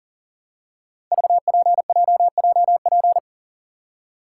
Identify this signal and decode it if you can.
Morse code, heard as a keyed tone, about 38 wpm, 700 Hz. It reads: VPJJP
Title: Code Groups - 5 Letters - 40wpm